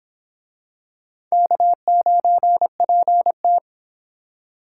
Morse code, heard as a keyed tone, about 26 wpm, 700 Hz. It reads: K9PT